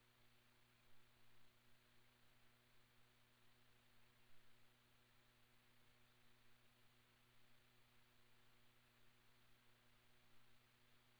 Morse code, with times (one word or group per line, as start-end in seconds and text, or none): none